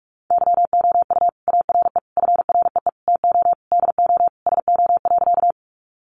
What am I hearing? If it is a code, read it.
Morse code, heard as a keyed tone, about 44 wpm, 700 Hz. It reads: YOU ARE FREE TO DO SO.
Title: Sentences from Top 500 Words 45wpm